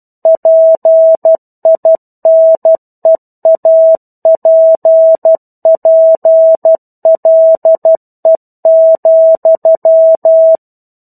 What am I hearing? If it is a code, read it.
Morse code, heard as a keyed tone, about 12 wpm, 650 Hz. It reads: PINEAPPLE,
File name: 01-Chapter-01 12 WPM